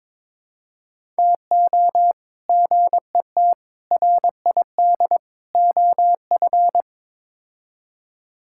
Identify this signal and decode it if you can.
Morse code, heard as a keyed tone, about 22 wpm, 700 Hz. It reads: TO GET RID OF
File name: practice / 22wpm / Build A Phrase - 5-Word Phrases - 22wpm